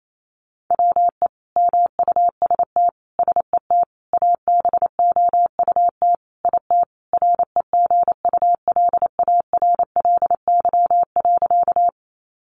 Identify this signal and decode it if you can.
Morse code, heard as a keyed tone, about 28 wpm, 700 Hz. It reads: WE MUST SET ABOUT IT REGULARLY.
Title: Sentences from Top 1400 Words 28wpm